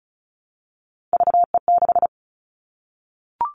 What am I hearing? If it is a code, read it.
Morse code, heard as a keyed tone, about 34 wpm, 700 Hz. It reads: VE6
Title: Callsign Prefixes for Top 50 DXCC - 35wpm